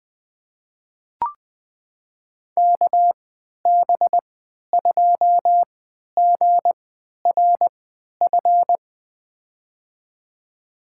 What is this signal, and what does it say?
Morse code, heard as a keyed tone, about 20 wpm, spaced as for 11 wpm, 700 Hz. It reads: KB2GRF
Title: USA Call Signs - 3x Character Spacing - 20wpm